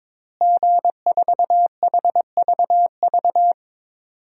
Morse code, heard as a keyed tone, about 22 wpm, 700 Hz. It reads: G4HVV
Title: DXCC-100 Call Signs - 5 Characters - Repeated 3x - 22wpm